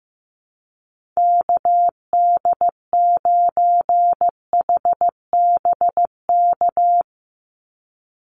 Morse code, heard as a keyed tone, about 15 wpm, 700 Hz. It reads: KD9HBK
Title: USA Call Signs 15wpm